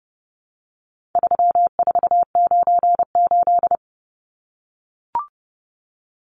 Morse code, heard as a keyed tone, about 30 wpm, 700 Hz. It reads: 3498
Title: Numbers - 4-Digit 30wpm